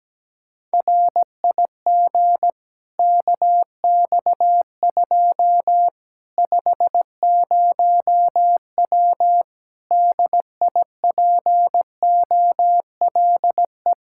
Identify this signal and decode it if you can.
Morse code, heard as a keyed tone, about 17 wpm, 700 Hz. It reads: RIG KX2 50W DIPOLE
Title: Practice QSO - Exchange 2 - Info - 17wpm